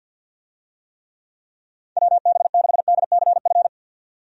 Morse code, heard as a keyed tone, about 47 wpm, 700 Hz. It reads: WB6DCF
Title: USA Call Signs - 6 Characters - 50wpm